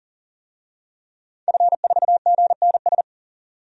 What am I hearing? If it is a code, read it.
Morse code, heard as a keyed tone, about 40 wpm, 700 Hz. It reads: F4GNS
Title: DXCC-100 Call Signs - 5 Characters - 40wpm